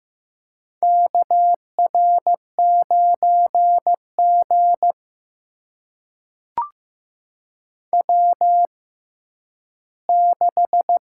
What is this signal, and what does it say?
Morse code, heard as a keyed tone, about 15 wpm, spaced as for 10 wpm, 700 Hz. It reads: KR9G W6
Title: USA Call Signs - 4 Characters - 6x Character Spacing - 15wpm